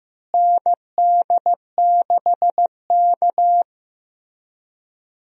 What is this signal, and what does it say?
Morse code, heard as a keyed tone, about 15 wpm, 700 Hz. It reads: ND6K
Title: USA Call Signs - 4 Characters - Repeated 3x - 15wpm